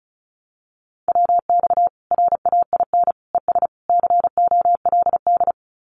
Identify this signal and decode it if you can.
Morse code, heard as a keyed tone, about 34 wpm, 700 Hz. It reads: WX RAIN ES COLD
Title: Common QSO Phrases 35wpm